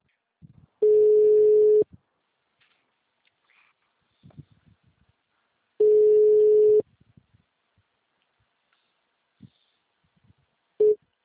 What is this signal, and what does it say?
Ringback tone (Europe, 425 Hz).